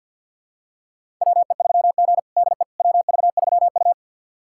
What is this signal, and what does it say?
Morse code, heard as a keyed tone, about 49 wpm, 700 Hz. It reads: WE3G DE WV3U
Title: USA Callsign DE Callsign - 4 Characters - 50wpm